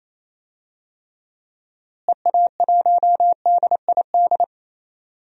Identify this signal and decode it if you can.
Morse code, heard as a keyed tone, about 28 wpm, 700 Hz. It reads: EA1DID
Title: DXCC-100 Call Signs - 6 Characters - Repeated 2x - 28wpm